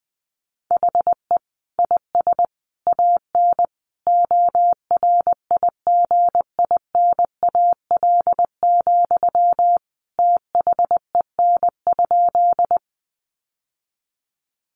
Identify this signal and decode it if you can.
Morse code, heard as a keyed tone, about 20 wpm, 700 Hz. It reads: HE IS AN ORIGINAL, THEN?
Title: Sentences from Top 900 Words 20wpm